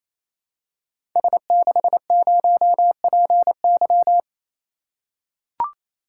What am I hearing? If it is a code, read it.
Morse code, heard as a keyed tone, about 29 wpm, 700 Hz. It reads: S60PY
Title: Code Groups - 5 Letter:Numbers - 28wpm